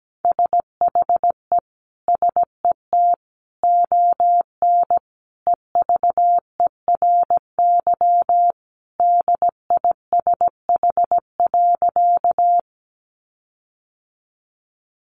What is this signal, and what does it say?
Morse code, heard as a keyed tone, about 17 wpm, 700 Hz. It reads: SHE SET ON EVERY DISH.
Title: Sentences from Top 1300 Words 17wpm